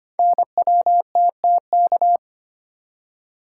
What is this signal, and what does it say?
Morse code, heard as a keyed tone, about 25 wpm, 700 Hz. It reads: NWTTK
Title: Code Groups - 5 Letters - 25wpm